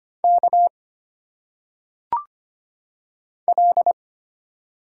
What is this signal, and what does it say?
Morse code, heard as a keyed tone, about 25 wpm, 700 Hz. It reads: K L
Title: Single Letters - Phonetic Pronunciation 25wpm